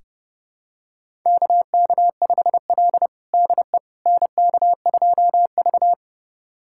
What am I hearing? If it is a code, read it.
Morse code, heard as a keyed tone, about 30 wpm, 700 Hz. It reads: KK5L DE NK2V
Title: USA Callsign DE Callsign - 4 Characters - 30wpm